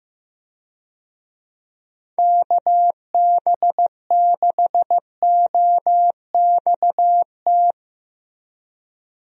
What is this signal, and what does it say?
Morse code, heard as a keyed tone, about 15 wpm, 700 Hz. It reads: KB6OXT
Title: USA Call Signs - 6 Characters - Repeated 2x - 15wpm